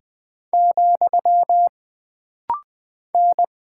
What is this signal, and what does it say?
Morse code, heard as a keyed tone, about 20 wpm, 700 Hz. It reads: , N
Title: Single Character (Letter:Number:Punctuation:Prosign) - Rapid-Fire - 20wpm